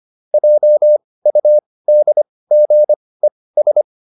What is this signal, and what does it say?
Morse code, heard as a keyed tone, about 25 wpm, spaced as for 18 wpm, 600 Hz. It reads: JUDGES